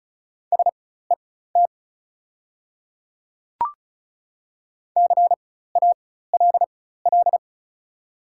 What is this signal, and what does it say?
Morse code, heard as a keyed tone, about 36 wpm, spaced as for 16 wpm, 700 Hz. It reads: SET CALL
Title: 4x Character Spacing - Top 200 Words - 35wpm